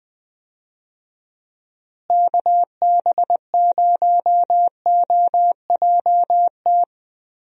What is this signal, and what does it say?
Morse code, heard as a keyed tone, about 20 wpm, 700 Hz. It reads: KB0OJT
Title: USA Call Signs - 6 Characters - Repeated 2x - 20wpm